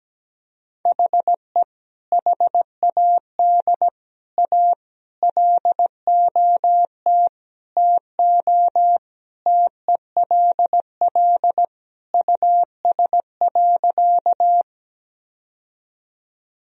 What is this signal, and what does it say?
Morse code, heard as a keyed tone, about 17 wpm, 700 Hz. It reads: HE HAD A LOT TO TELL US.
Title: Sentences from Top 1000 Words - Review Part 3 - 17wpm